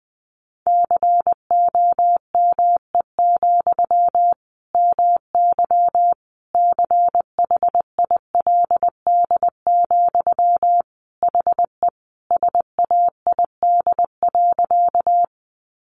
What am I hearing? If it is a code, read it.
Morse code, heard as a keyed tone, about 20 wpm, 700 Hz. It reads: COME, MY CHILD, HE SAID.
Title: Sentences from Top 800 Words 20wpm